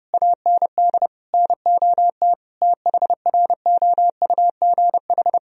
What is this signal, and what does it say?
Morse code, heard as a keyed tone, about 32 wpm, 700 Hz. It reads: AND NOT THROUGH
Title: Sentences from Top 1000 Words - Review Part 1 - 30wpm